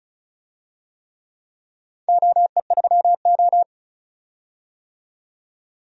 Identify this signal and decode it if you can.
Morse code, heard as a keyed tone, about 35 wpm, 700 Hz. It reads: OE3O
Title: DXCC-100 Call Signs - 4 Characters - 35wpm